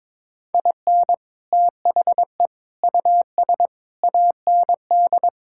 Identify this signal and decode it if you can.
Morse code, heard as a keyed tone, about 22 wpm, 700 Hz. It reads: IN THE US AND
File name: practice / 22wpm / Sets of 7 Words - 22wpm